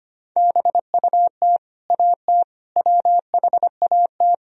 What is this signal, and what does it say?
Morse code, heard as a keyed tone, about 25 wpm, 700 Hz. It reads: BUT AT WHAT
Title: Sentences from Top 1000 Words - Review Part 4 - 25wpm